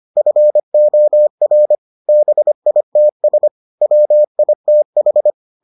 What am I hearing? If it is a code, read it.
Morse code, heard as a keyed tone, about 25 wpm, 600 Hz. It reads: FOR BITS WITH